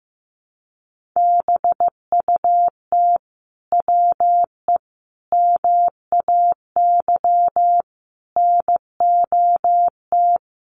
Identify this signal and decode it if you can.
Morse code, heard as a keyed tone, about 15 wpm, 700 Hz. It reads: BUT WE MAY NOT
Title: Sentences from Top 1500 Words 15wpm